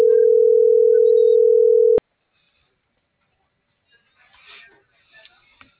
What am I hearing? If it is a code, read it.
Ringback tone (North America), 440 + 480 Hz.